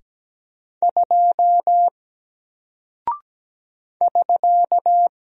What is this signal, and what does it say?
Morse code, heard as a keyed tone, about 17 wpm, 700 Hz. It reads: 2 <SK>
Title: Single Character (Letter:Number:Punctuation:Prosign) Alt 17wpm